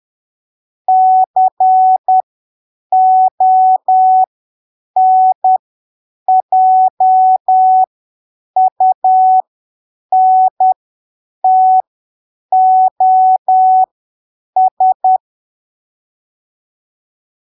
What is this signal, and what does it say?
Morse code, heard as a keyed tone, about 10 wpm, spaced as for 7 wpm, 750 Hz. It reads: CONJUNTOS
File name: arquivos / texto / texto 13 10ppm